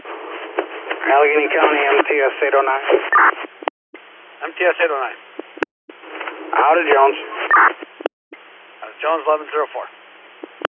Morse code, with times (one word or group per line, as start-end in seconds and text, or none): none